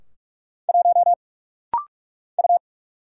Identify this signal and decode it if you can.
Morse code, heard as a keyed tone, about 44 wpm, 700 Hz. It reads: 1 U
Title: Single Letter-Number Alt - Rapid-Fire 45wpm